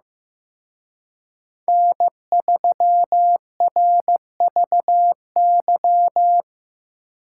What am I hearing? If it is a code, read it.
Morse code, heard as a keyed tone, about 18 wpm, 700 Hz. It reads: N3RVY
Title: Speed Racing - USA Call Signs 15wpm